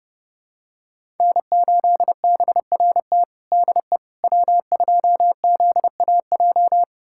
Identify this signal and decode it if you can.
Morse code, heard as a keyed tone, about 30 wpm, 700 Hz. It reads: N8BRT DE W2ZAJ
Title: USA Callsign DE Callsign - 5 Characters - 30wpm